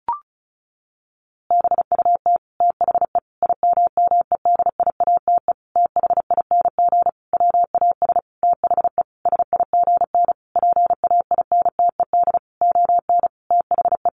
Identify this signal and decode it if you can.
Morse code, heard as a keyed tone, about 34 wpm, 700 Hz. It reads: BUT THE IMMEDIATE THING WAS THE SIGN PAINTED ON THE
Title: A Sound Of Thunder 35wpm